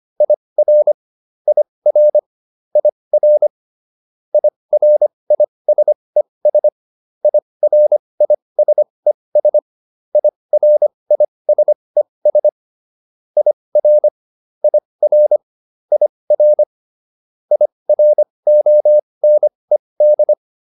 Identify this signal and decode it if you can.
Morse code, heard as a keyed tone, about 25 wpm, spaced as for 20 wpm, 600 Hz. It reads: IR IR IR IRISES IRISES IRISES IR IR IR IRONED
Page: advanced-practice-files